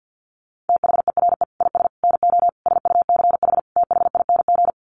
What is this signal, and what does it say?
Morse code, heard as a keyed tone, about 50 wpm, 700 Hz. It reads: THERE IS NO SUCH THING
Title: Sets of 5 Words Encore 50wpm